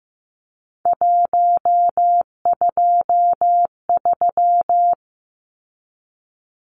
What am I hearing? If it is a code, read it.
Morse code, heard as a keyed tone, about 15 wpm, 700 Hz. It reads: 123
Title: Numbers - 3-Digit 15wpm